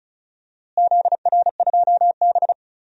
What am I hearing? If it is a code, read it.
Morse code, heard as a keyed tone, about 36 wpm, 700 Hz. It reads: ZR2B
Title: DXCC-100 Call Signs - 4 Characters - Repeated 2x - 35wpm